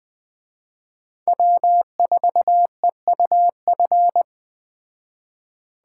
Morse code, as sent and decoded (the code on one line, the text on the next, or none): .-- ....- . ..- ..-.
W4EUF